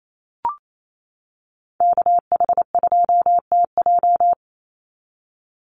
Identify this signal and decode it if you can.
Morse code, heard as a keyed tone, about 28 wpm, 700 Hz. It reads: KH2TJ
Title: NA SOTA Activator Callsigns 28wpm